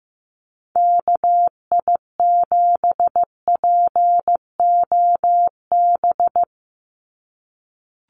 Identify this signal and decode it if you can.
Morse code, heard as a keyed tone, about 15 wpm, 700 Hz. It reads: KI7POB
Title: USA Call Signs 15wpm